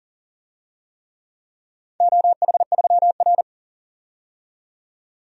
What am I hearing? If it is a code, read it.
Morse code, heard as a keyed tone, about 40 wpm, 700 Hz. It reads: OH3R
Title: DXCC-100 Call Signs - 4 Characters - 40wpm